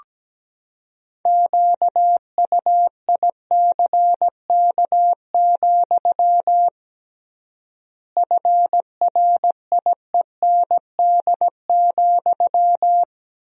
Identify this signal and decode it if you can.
Morse code, heard as a keyed tone, about 17 wpm, 700 Hz. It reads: QUICK, FRIEND,
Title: 3x Word-Spacing - Sentences from Top 500 Words - 17wpm